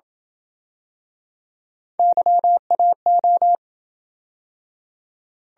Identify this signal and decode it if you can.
Morse code, heard as a keyed tone, about 28 wpm, 700 Hz. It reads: YAO